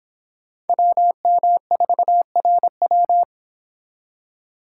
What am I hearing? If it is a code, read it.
Morse code, heard as a keyed tone, about 26 wpm, 700 Hz. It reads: WM4RW